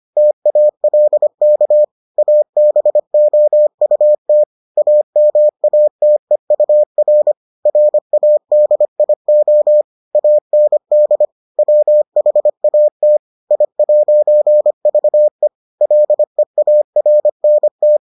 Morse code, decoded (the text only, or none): TALK ABOUT AMATEUR RADIO AND WHAT I'VE LEARNT